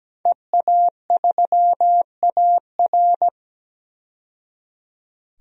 Morse code, heard as a keyed tone, about 17 wpm, 700 Hz. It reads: EA3AR
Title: DXCC-100 Call Signs - 5 Characters - 17wpm